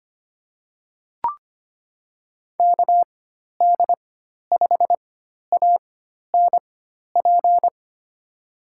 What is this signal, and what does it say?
Morse code, heard as a keyed tone, about 25 wpm, spaced as for 12 wpm, 700 Hz. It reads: KD5ANP
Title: USA Call Signs - 6 Characters - 4x Character Spacing - 25wpm